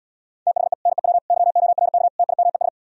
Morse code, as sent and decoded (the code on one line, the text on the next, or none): - .... / -. .-- / -.-- --- .-. -.- / - .. -- . ...
THE NEW YORK TIMES